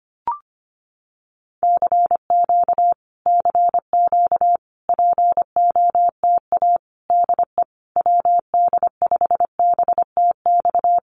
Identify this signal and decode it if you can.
Morse code, heard as a keyed tone, about 25 wpm, 700 Hz. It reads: CQ CQ POTA DE WD5BTX
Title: Common QSO Phrases Encore 25wpm